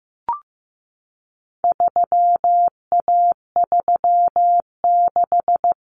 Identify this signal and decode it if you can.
Morse code, heard as a keyed tone, about 15 wpm, 700 Hz. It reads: 3A36
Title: Common Cut Numbers - 4-Digit 15wpm